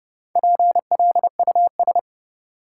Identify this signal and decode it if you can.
Morse code, heard as a keyed tone, about 30 wpm, 700 Hz. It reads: PLUS
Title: Four Letter Words - 30wpm